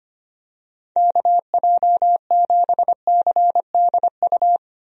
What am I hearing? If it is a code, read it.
Morse code, heard as a keyed tone, about 25 wpm, 700 Hz. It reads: KJ7CDU
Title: USA Call Signs - 6 Characters - 25wpm